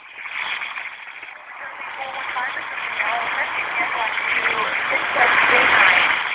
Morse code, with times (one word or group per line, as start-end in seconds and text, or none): none